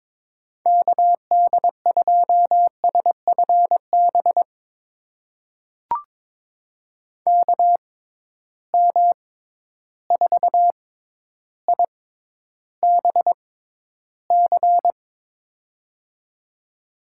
Morse code, heard as a keyed tone, about 22 wpm, spaced as for 11 wpm, 700 Hz. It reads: KD2SFB KM4IBC
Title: USA Call Signs - 6 Characters - 6x Character Spacing - 22wpm